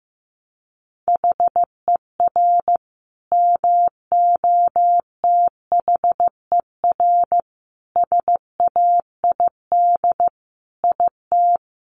Morse code, heard as a keyed tone, about 15 wpm, 700 Hz. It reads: HER MOTHER SAID IT
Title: Sentences from Top 1200 Words 15wpm